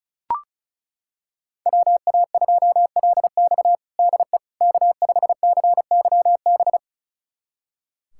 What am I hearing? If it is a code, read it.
Morse code, heard as a keyed tone, about 34 wpm, 700 Hz. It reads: WA2LX DE K5CYB
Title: USA Callsign DE Callsign - 5 Characters - 35wpm